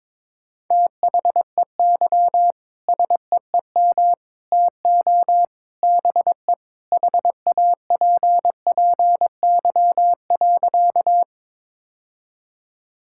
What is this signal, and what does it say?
Morse code, heard as a keyed tone, about 22 wpm, 700 Hz. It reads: THEY SEEM TO BE HAPPY.
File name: practice / 22wpm / Sentences with 3-Word Phrases - 22wpm